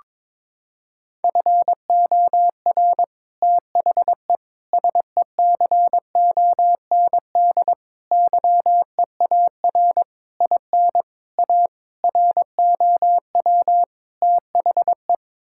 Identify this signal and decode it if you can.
Morse code, heard as a keyed tone, about 22 wpm, 700 Hz. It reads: FOR THE SECOND YEAR IN A ROW THE
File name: practice / 22wpm / Sets of 8 Words - 22wpm